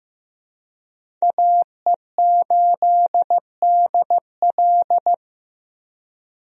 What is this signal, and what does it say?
Morse code, heard as a keyed tone, about 15 wpm, 700 Hz. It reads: AE8DL
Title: USA Call Signs - 5 Characters - 15wpm